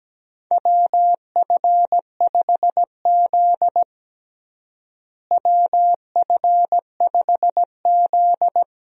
Morse code, heard as a keyed tone, about 17 wpm, 700 Hz. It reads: WF5Z WF5Z
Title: USA Call Signs - 4 Characters - Repeated 3x - 17wpm